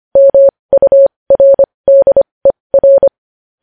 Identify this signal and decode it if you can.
Morse code, heard as a keyed tone, about 25 wpm, spaced as for 20 wpm, 550 Hz. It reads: MURDER
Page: News Headlines in Morse Code at 20 WPM